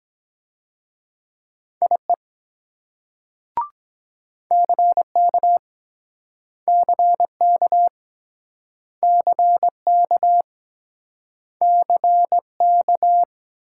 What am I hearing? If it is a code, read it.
Morse code, heard as a keyed tone, about 22 wpm, 700 Hz. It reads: IE CK CK CK CK